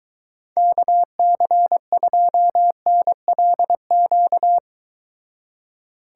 Morse code, heard as a keyed tone, about 23 wpm, 700 Hz. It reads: KC2NLQ